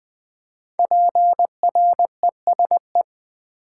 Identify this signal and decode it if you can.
Morse code, heard as a keyed tone, about 20 wpm, 700 Hz. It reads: PRESE
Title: Build A Word - 5 Letters At A Time - 12-Letter Words - 20wpm